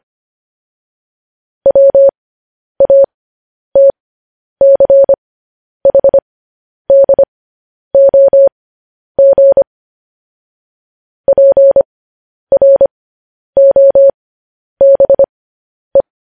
Morse code, heard as a keyed tone, about 25 wpm, spaced as for 10 wpm, 550 Hz. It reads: WATCHDOG PROBE
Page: news headlines in morse code at 10 wpm